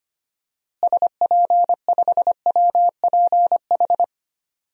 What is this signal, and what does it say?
Morse code, heard as a keyed tone, about 25 wpm, 700 Hz. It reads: SP5WPH
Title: DXCC-100 Call Signs - 6 Characters - Repeated 3x - 25wpm